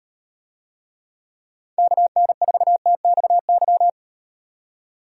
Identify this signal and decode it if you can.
Morse code, heard as a keyed tone, about 38 wpm, 700 Hz. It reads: KN4TXY